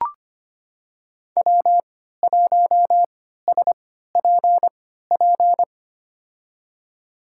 Morse code, heard as a keyed tone, about 25 wpm, spaced as for 14 wpm, 700 Hz. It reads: W1SPP
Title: USA Call Signs - 5 Characters - 3x Character Spacing - 25wpm